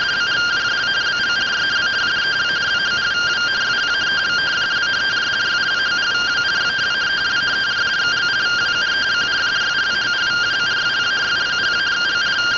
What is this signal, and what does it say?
Two-tone teleprinter signal, about 170 Hz shift.